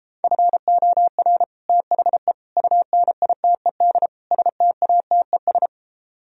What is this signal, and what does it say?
Morse code, heard as a keyed tone, about 29 wpm, 700 Hz. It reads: FOR THE UNITED STATES